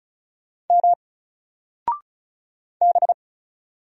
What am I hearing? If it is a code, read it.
Morse code, heard as a keyed tone, about 36 wpm, 700 Hz. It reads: M B
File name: Single-Letter-Number-35wpm